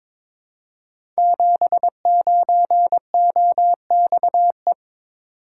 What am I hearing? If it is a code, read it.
Morse code, heard as a keyed tone, about 23 wpm, 700 Hz. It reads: 79OXE
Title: Code Groups - 5 Letter:Numbers - 22wpm